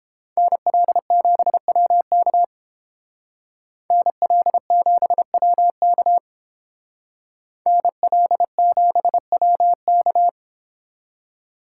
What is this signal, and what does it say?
Morse code, heard as a keyed tone, about 28 wpm, 700 Hz. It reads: NL7WK NL7WK NL7WK